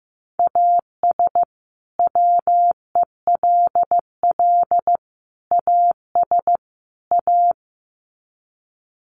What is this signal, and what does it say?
Morse code, heard as a keyed tone, about 14 wpm, 700 Hz. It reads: AS WELL AS A
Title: Sets of 4 Words 15wpm